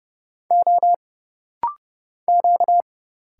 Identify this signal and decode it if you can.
Morse code, heard as a keyed tone, about 30 wpm, 700 Hz. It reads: O Q
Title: Single Letter-Number Alt - Rapid-Fire 30wpm